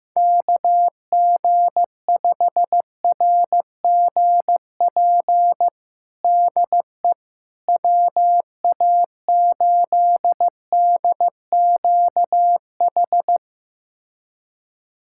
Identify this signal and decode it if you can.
Morse code, heard as a keyed tone, about 15 wpm, 700 Hz. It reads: KG5RGP DE WA8DQH
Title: USA Callsign DE Callsign 15wpm